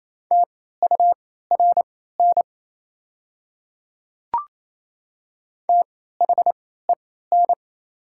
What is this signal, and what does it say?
Morse code, heard as a keyed tone, about 27 wpm, spaced as for 16 wpm, 700 Hz. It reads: TURN THEN
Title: 3x Character Spacing - Top 200 Words - 28wpm